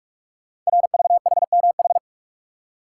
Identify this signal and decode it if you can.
Morse code, heard as a keyed tone, about 44 wpm, 700 Hz. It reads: RVHMH
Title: Code Groups - 5 Letters - 45wpm